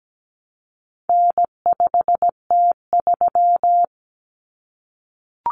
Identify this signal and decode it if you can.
Morse code, heard as a keyed tone, about 17 wpm, 700 Hz. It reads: N5T3
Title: Common Cut Numbers - 4-Digit 17wpm